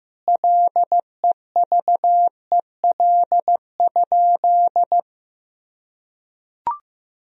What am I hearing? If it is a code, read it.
Morse code, heard as a keyed tone, about 15 wpm, 700 Hz. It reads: LEVEL?
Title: Sentences from Top 1000 Words - Review Part 4 - 15wpm